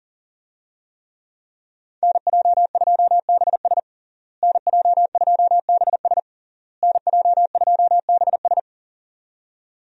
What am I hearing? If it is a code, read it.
Morse code, heard as a keyed tone, about 37 wpm, 700 Hz. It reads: NJ2BS NJ2BS NJ2BS
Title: USA Call Signs - 5 Characters - Repeated 3x - 40wpm